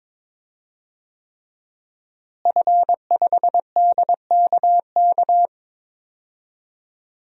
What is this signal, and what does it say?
Morse code, heard as a keyed tone, about 22 wpm, 700 Hz. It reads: F5DKK
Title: DXCC-100 Call Signs - 5 Characters - Repeated 3x - 22wpm